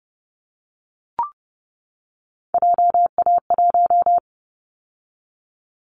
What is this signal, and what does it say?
Morse code, heard as a keyed tone, about 30 wpm, 700 Hz. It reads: JA1
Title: Callsign Prefixes for Top 50 DXCC - 30wpm